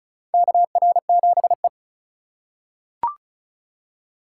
Morse code, heard as a keyed tone, about 36 wpm, 700 Hz. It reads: KR7E
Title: USA Call Signs - 4 Characters - 4x Character Spacing - 35wpm